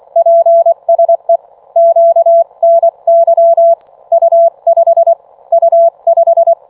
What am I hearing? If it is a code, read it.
Morse code, heard as a keyed tone, about 24 wpm, 670 Hz. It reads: PSE QNY U5 U5